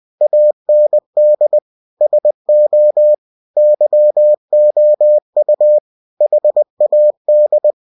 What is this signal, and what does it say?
Morse code, heard as a keyed tone, about 20 wpm, 600 Hz. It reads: AND SO YOU HAD
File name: cw-books / winnie-the-pooh / winnie-20wpm-page-0005